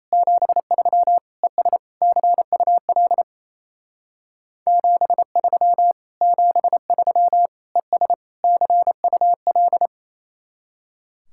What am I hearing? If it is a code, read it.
Morse code, heard as a keyed tone, about 35 wpm, 700 Hz. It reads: 73 ES CUL 73 73 ES CUL
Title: Speed-Racing Common QSO Phrases - 28wpm